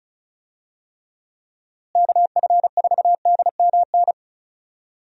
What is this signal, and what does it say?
Morse code, heard as a keyed tone, about 36 wpm, 700 Hz. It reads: KF4DMN
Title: USA Call Signs - 6 Characters - Repeated 3x - 35wpm